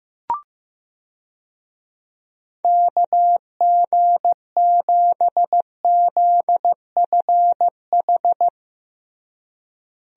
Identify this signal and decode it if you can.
Morse code, heard as a keyed tone, about 15 wpm, 700 Hz. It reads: KG7ZFH
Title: USA Call Signs - 6 Characters - Repeated 2x - 15wpm